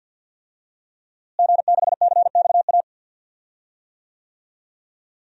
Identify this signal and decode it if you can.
Morse code, heard as a keyed tone, about 50 wpm, 700 Hz. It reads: G6CXA
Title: DXCC-100 Call Signs - 5 Characters - Repeated 2x - 50wpm